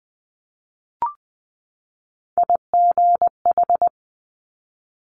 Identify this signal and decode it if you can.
Morse code, heard as a keyed tone, about 20 wpm, 700 Hz. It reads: IGH
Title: Top Three Letter Combinations in Words 20wpm